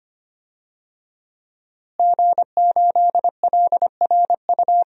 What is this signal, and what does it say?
Morse code, heard as a keyed tone, about 25 wpm, 700 Hz. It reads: G8LRU
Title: DXCC-100 Call Signs - 5 Characters - Repeated 3x - 25wpm